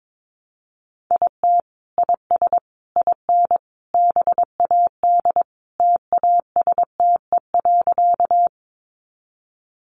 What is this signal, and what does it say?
Morse code, heard as a keyed tone, about 22 wpm, 700 Hz. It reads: IT IS IN BAD TASTE.
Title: Sentences from Top 1100 Words 22wpm